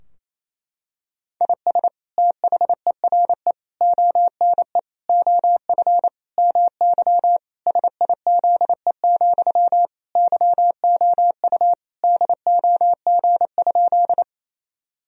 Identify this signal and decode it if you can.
Morse code, heard as a keyed tone, about 27 wpm, 700 Hz. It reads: IS THERE ONE OF MY SIZE, YOU DOG?
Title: Sentences from Top 1000 Words - Review Part 2 - 28wpm